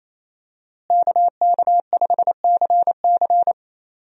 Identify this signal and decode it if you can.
Morse code, heard as a keyed tone, about 28 wpm, 700 Hz. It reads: KK5CC
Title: USA Call Signs - 5 Characters - Repeated 3x - 28wpm